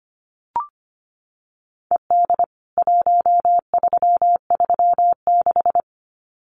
Morse code, heard as a keyed tone, about 25 wpm, 700 Hz. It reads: ED 1336
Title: CWT Exchange 25wpm